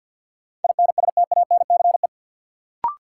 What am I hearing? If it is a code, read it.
Morse code, heard as a keyed tone, about 54 wpm, 700 Hz. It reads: INSTANCE
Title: Sentences from Top 800 Words - New Word Drill - 50wpm